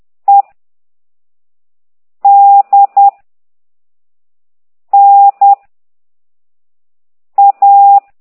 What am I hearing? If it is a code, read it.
Morse code, heard as a keyed tone, about 10 wpm, 800 Hz. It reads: E D N A
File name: ct2idw / exerc13